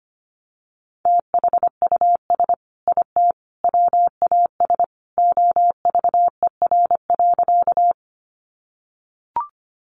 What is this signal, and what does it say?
Morse code, heard as a keyed tone, about 25 wpm, 700 Hz. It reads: THUS IT WAS OVER.
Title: Sentences from Top 800 Words 25wpm